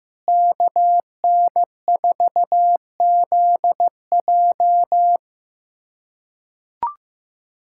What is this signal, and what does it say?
Morse code, heard as a keyed tone, about 15 wpm, 700 Hz. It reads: KN4ZJ
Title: USA Callsign DE Callsign - 5 Characters - 15wpm